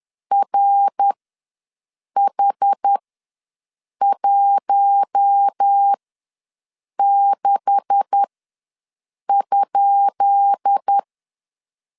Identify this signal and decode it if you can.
Morse code, heard as a keyed tone, about 10 wpm, spaced as for 6 wpm, 800 Hz. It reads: RH16?
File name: LEZIONI / 10 SIGNS / 10B INT ws 10wpm 4134